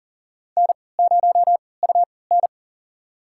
Morse code, heard as a keyed tone, about 37 wpm, spaced as for 22 wpm, 700 Hz. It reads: N0UN
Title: USA Call Signs - 3x Character Spacing - 40wpm